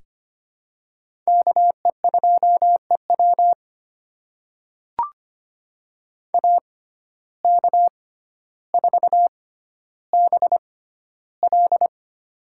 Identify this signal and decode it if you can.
Morse code, heard as a keyed tone, about 25 wpm, spaced as for 13 wpm, 700 Hz. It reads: KE2EW AK4BL
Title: USA Call Signs - 5 Characters - 6x Character Spacing - 25wpm